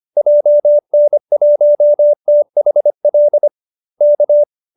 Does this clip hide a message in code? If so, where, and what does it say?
Yes: Morse code from 0.2 to 4.4 s, heard as a keyed tone, about 25 wpm, 590 Hz. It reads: JN1THL K